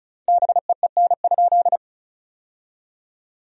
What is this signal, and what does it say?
Morse code, heard as a keyed tone, about 36 wpm, 700 Hz. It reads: BEEN?
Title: 4x Word-Spacing - Sentences from Top 400 Words - 35wpm